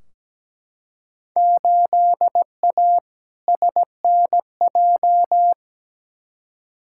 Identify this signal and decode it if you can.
Morse code, heard as a keyed tone, about 17 wpm, 700 Hz. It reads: 8A SNJ
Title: Field Day Exchanges Encore - 17wpm